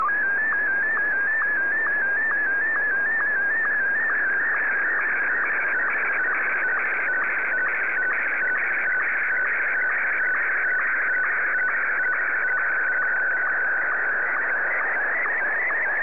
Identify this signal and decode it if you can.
SSTV (slow-scan television), Martin 1.